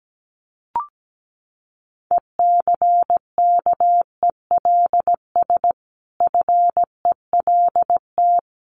Morse code, heard as a keyed tone, about 17 wpm, 700 Hz. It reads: ECKELS FELT
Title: A Sound Of Thunder 17wpm